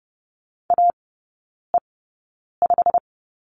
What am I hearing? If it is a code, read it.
Morse code, heard as a keyed tone, about 30 wpm, spaced as for 9 wpm, 700 Hz. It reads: AE5
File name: course / 30:10wpm / 087 Lesson 15e - Number 5 - Three Characters 30wpm